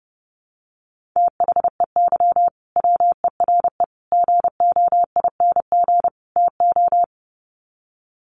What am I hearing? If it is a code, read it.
Morse code, heard as a keyed tone, about 30 wpm, 700 Hz. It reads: THEY WERE GOING TO
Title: Sets of 4 Words 30wpm